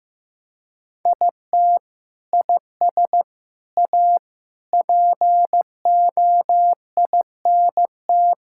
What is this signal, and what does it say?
Morse code, heard as a keyed tone, about 15 wpm, 700 Hz. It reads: IT IS A POINT
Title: Sentences from Top 1000 Words - Review Part 5 - 15wpm